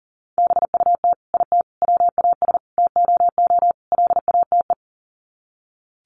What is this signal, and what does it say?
Morse code, heard as a keyed tone, about 40 wpm, 700 Hz. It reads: BUT IT WAS TOO LATE
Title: Sets of 5 Words Encore2 40wpm